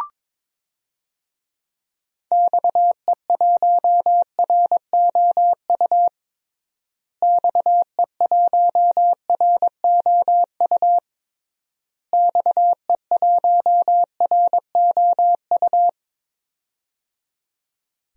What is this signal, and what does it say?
Morse code, heard as a keyed tone, about 22 wpm, 700 Hz. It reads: XE1ROU XE1ROU XE1ROU
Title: DXCC-100 Call Signs - 6 Characters - Repeated 3x - 22wpm